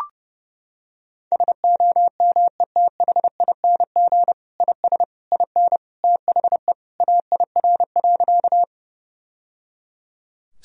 Morse code, heard as a keyed tone, about 30 wpm, 700 Hz. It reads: SOMETHING IS IN THE AIR.
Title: Sentences from Top 1000 Words - Review Part 1 - 30wpm